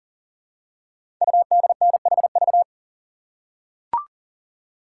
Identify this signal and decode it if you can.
Morse code, heard as a keyed tone, about 38 wpm, 700 Hz. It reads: UDNHV
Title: Code Groups - 5 Letters - 40wpm